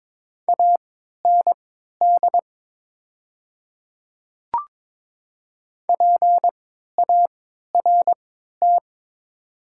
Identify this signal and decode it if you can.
Morse code, heard as a keyed tone, about 22 wpm, spaced as for 13 wpm, 700 Hz. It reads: AND PART
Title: 3x Character Spacing - Top 300 Words - 22wpm